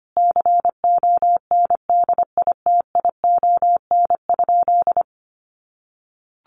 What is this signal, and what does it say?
Morse code, heard as a keyed tone, about 25 wpm, 700 Hz. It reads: CONDITION?
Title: Sentences from Top 1300 Words 25wpm